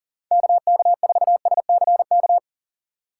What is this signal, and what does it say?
Morse code, heard as a keyed tone, about 40 wpm, 700 Hz. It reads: KK4SCK
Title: USA Call Signs - 3x Character Spacing - 40wpm